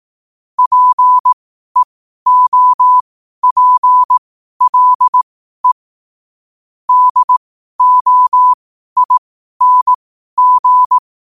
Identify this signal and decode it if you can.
Morse code, heard as a keyed tone, about 18 wpm, spaced as for 12 wpm, 1000 Hz. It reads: PEOPLE DOING
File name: qotdfiles / 12wpm